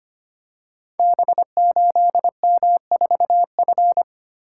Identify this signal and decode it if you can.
Morse code, heard as a keyed tone, about 25 wpm, 700 Hz. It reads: B8M4F
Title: Code Groups - 5 Letter:Numbers - 25wpm